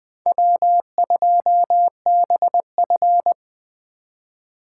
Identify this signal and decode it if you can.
Morse code, heard as a keyed tone, about 21 wpm, 700 Hz. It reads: W2BF
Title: Speed Racing - USA 1x2 Call Signs 17wpm